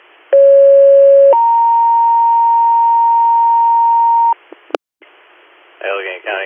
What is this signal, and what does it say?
Two-tone page, heard as a one-second tone, then a three-second tone, about 550 then 930 Hz, with speech. It is heard from 0.3 to 4.3 s.